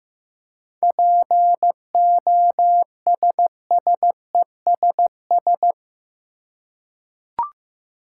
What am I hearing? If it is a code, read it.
Morse code, heard as a keyed tone, about 15 wpm, 700 Hz. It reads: POSSESS
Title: Seven Letter Words - 15wpm